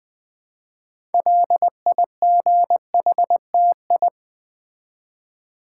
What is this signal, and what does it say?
Morse code, heard as a keyed tone, about 20 wpm, 700 Hz. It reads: LIGHTI
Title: Build A Word - 3 Letters At A Time - 8-Letter Words - 20wpm